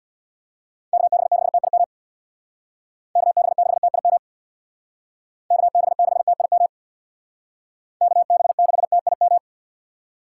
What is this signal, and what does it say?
Morse code, heard as a keyed tone, about 63 wpm, 700 Hz. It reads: KB6TIM KB6TIM KB6TIM KB6TIM